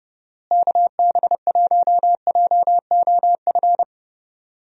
Morse code, heard as a keyed tone, about 31 wpm, 700 Hz. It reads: KB1JOF
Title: USA Call Signs - 6 Characters - Repeated 2x - 30wpm